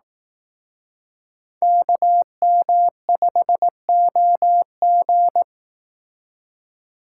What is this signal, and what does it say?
Morse code, heard as a keyed tone, about 18 wpm, 700 Hz. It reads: KM5OG
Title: Speed Racing - USA 2x2 Call Signs 15wpm